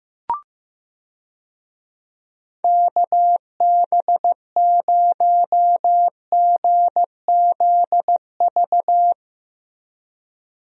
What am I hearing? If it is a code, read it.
Morse code, heard as a keyed tone, about 15 wpm, 700 Hz. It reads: KB0GZV
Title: USA Call Signs - 6 Characters - Repeated 3x - 15wpm